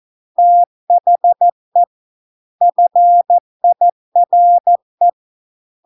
Morse code, heard as a keyed tone, about 14 wpm, 700 Hz. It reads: THE FIRE